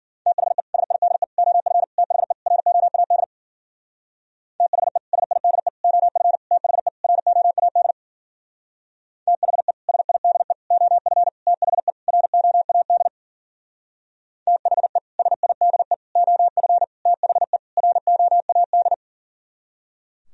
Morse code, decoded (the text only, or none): THE SIDE OF THE ROAD THE SIDE OF THE ROAD THE SIDE OF THE ROAD THE SIDE OF THE ROAD